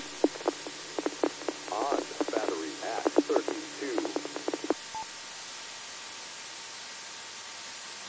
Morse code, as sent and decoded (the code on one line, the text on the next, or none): none